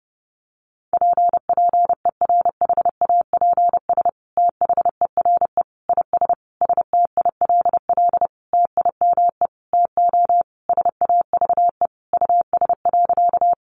Morse code, heard as a keyed tone, about 30 wpm, 700 Hz. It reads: PPERHAPS THERE IS STILL TIME TO SAVE US.
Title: Sentences from Top 1100 Words 30wpm